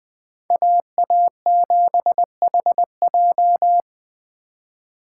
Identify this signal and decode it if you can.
Morse code, heard as a keyed tone, about 20 wpm, 700 Hz. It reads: AA7HJ